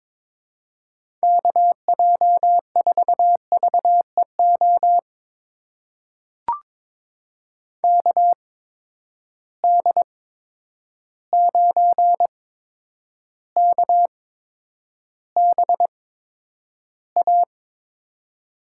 Morse code, heard as a keyed tone, about 22 wpm, spaced as for 9 wpm, 700 Hz. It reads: KJ4VEO KD9KBA
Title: USA Call Signs - 8x Character Spacing - 22wpm